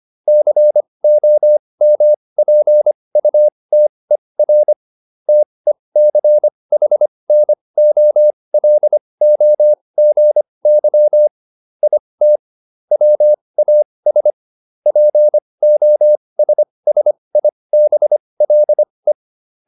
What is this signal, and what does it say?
Morse code, heard as a keyed tone, about 25 wpm, spaced as for 20 wpm, 600 Hz. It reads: COMPUTER TECHNOLOGY IT WAS POSSIBLE
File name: SS315 20